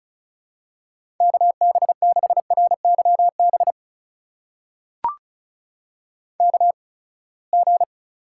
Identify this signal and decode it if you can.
Morse code, heard as a keyed tone, about 36 wpm, spaced as for 24 wpm, 700 Hz. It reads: KB6RYB KG
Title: USA Call Signs - 6 Characters - 8x Character Spacing - 35wpm